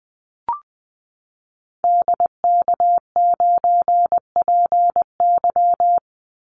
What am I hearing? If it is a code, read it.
Morse code, heard as a keyed tone, about 20 wpm, 700 Hz. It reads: DK9PY
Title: Call Signs Repeated 2x - 20wpm